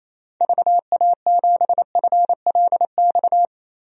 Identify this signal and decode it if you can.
Morse code, heard as a keyed tone, about 28 wpm, 700 Hz. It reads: VA7FLX
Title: DXCC-100 Call Signs - 6 Characters - Repeated 2x - 28wpm